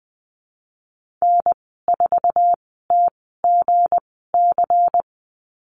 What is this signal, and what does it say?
Morse code, heard as a keyed tone, about 20 wpm, spaced as for 15 wpm, 700 Hz. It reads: N4TGC
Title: 186 Lesson 30g - Letter b - Call Signs 20wpm